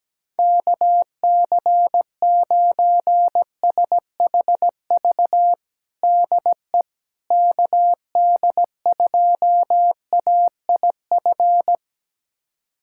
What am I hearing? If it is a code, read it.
Morse code, heard as a keyed tone, about 17 wpm, 700 Hz. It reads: KC9SHV DE KD2AIF
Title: USA Callsign DE Callsign - 6 Characters - 17wpm